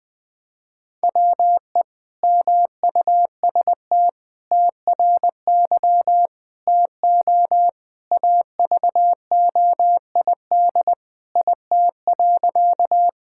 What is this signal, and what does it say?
Morse code, heard as a keyed tone, about 20 wpm, 700 Hz. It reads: WE MUST TRY TO AVOID IT.
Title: Sentences from Top 1000 Words - Review Part 6 - 20wpm